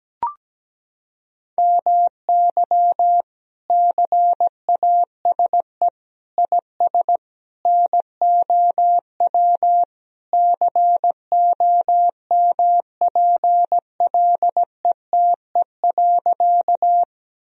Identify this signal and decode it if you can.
Morse code, heard as a keyed tone, about 17 wpm, 700 Hz. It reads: MY CASE IS NOW COMPLETE.
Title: Sentences from Top 1000 Words - Review Part 2 - 17wpm